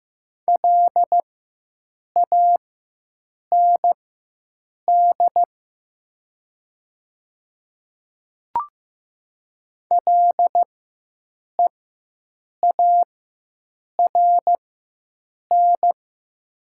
Morse code, heard as a keyed tone, about 15 wpm, spaced as for 7 wpm, 700 Hz. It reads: LAND LEARN
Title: 4x Character Spacing - Top 300 Words - 15wpm